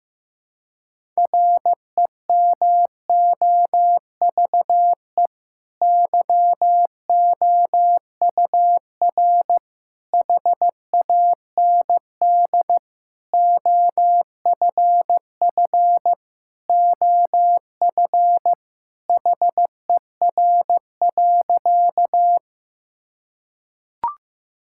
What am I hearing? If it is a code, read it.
Morse code, heard as a keyed tone, about 16 wpm, 700 Hz. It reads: REMOVE YOUR HAND OFF OF HER.
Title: Sentences from Top 1000 Words - Review Part 5 - 15wpm